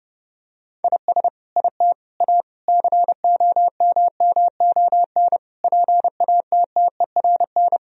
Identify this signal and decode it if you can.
Morse code, heard as a keyed tone, about 30 wpm, 700 Hz. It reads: IS IT A COMMON PATTERN
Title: Sentences from Top 1000 Words - Review Part 2 - 30wpm